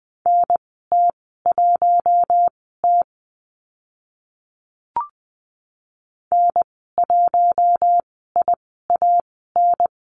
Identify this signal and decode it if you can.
Morse code, heard as a keyed tone, about 20 wpm, spaced as for 15 wpm, 700 Hz. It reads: NT1T N1IAN
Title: 042 Lesson 8g - Number 1 - Call Signs 20wpm